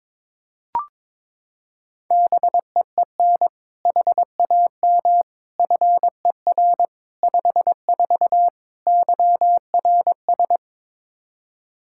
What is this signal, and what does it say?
Morse code, heard as a keyed tone, about 21 wpm, 700 Hz. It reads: BEEN HAM FER 54 YRS
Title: Practice QSO - Exchange 3 - More Info - 22wpm